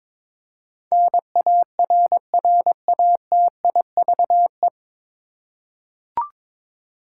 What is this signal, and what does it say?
Morse code, heard as a keyed tone, about 21 wpm, 700 Hz. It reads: NARRATIVE
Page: Morse Code Ninja practice 22wpm